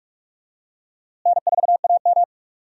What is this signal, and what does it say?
Morse code, heard as a keyed tone, about 46 wpm, 700 Hz. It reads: N4AM
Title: USA Call Signs - 4 Characters - Repeated 3x - 45wpm